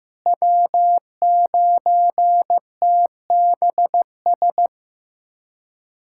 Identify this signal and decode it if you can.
Morse code, heard as a keyed tone, about 15 wpm, 700 Hz. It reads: W9TBS
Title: USA Callsign DE Callsign - 5 Characters - 15wpm